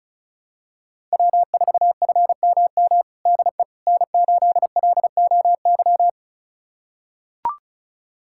Morse code, heard as a keyed tone, about 34 wpm, 700 Hz. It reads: W4FMM DE N8LOY
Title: USA Callsign DE Callsign - 5 Characters - 35wpm